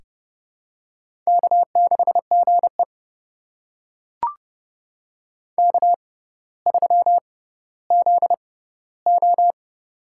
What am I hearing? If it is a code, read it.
Morse code, heard as a keyed tone, about 30 wpm, spaced as for 15 wpm, 700 Hz. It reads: K6GE K3ZO
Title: USA Call Signs - 4 Characters - 6x Character Spacing - 30wpm